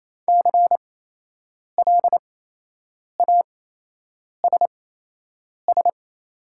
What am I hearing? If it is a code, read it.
Morse code, heard as a keyed tone, about 28 wpm, spaced as for 8 wpm, 700 Hz. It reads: CLASS
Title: 8x Character Spacing - Top 500 Words - 28wpm